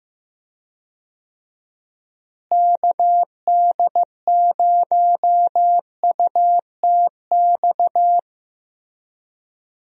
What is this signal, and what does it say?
Morse code, heard as a keyed tone, about 15 wpm, 700 Hz. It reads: KD0UTX
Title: USA Call Signs - 6 Characters - Repeated 2x - 15wpm